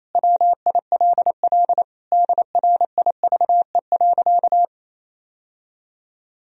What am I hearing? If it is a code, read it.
Morse code, heard as a keyed tone, about 28 wpm, 700 Hz. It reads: WILL DRIVE.
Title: Sentences from Top 1000 Words - Review Part 2 - 28wpm